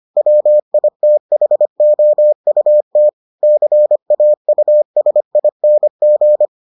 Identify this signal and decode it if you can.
Morse code, heard as a keyed tone, about 25 wpm, 600 Hz. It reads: WITHOUT CAUSING